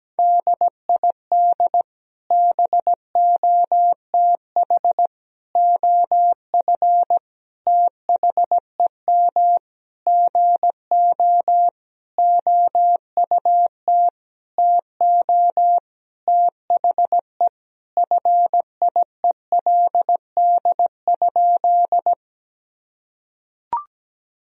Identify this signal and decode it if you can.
Morse code, heard as a keyed tone, about 17 wpm, 700 Hz. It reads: DID BOTH OF THEM GO OUT TO THE FIELD?
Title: Sentences from Top 1000 Words - Review Part 2 - 17wpm